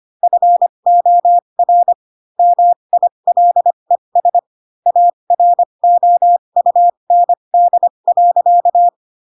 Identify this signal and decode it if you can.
Morse code, heard as a keyed tone, about 25 wpm, spaced as for 22 wpm, 700 Hz. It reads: FOR MILES AROUND.
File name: Pinocchio / Pinocchio ch7 18 25 ik1hge